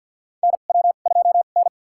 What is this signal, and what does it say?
Morse code, heard as a keyed tone, about 51 wpm, spaced as for 36 wpm, 700 Hz. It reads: NW2N
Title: USA Call Signs - 4 Characters - 2x Character Spacing - 50wpm